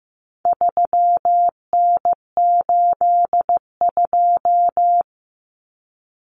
Morse code, heard as a keyed tone, about 15 wpm, 700 Hz. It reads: 3N82
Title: Common Cut Numbers - 4-Digit 15wpm